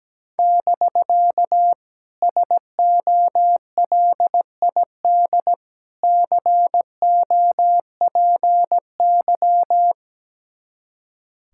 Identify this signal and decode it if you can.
Morse code, heard as a keyed tone, about 17 wpm, 700 Hz. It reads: <BK> SOLID COPY
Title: Speed-Racing Common QSO Phrases - 17wpm